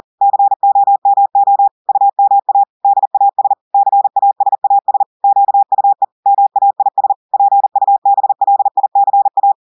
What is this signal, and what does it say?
Morse code, heard as a keyed tone, about 40 wpm, 800 Hz. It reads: COMO UMA DAS CASAS QUE MAIS PUBLICA
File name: audios / portugais / portugais texte en clair 1 vitesse 40 mots minutes